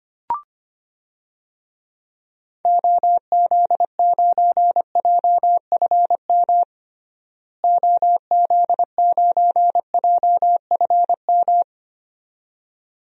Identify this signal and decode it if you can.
Morse code, heard as a keyed tone, about 25 wpm, 700 Hz. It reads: OZ9JFM OZ9JFM
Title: DXCC-100 Call Signs - 6 Characters - Repeated 2x - 25wpm